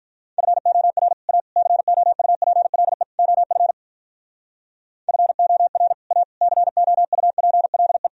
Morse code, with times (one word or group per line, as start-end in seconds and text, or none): 0.4-1.1 FOR
1.3-1.4 A
1.6-3.0 COUPLE
3.2-3.7 OF
5.1-5.9 FOR
6.1-6.2 A
6.4-8.1 COUPLE